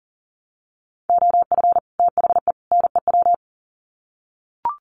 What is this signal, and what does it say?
Morse code, heard as a keyed tone, about 40 wpm, 700 Hz. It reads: OF THE NEW
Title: Sets of 3 Words 40wpm